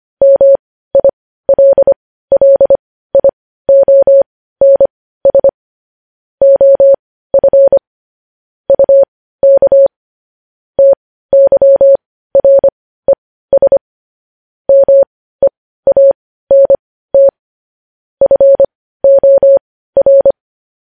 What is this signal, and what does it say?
Morse code, heard as a keyed tone, about 25 wpm, spaced as for 15 wpm, 550 Hz. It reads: MILLIONS OF UK TYRES MEANT FOR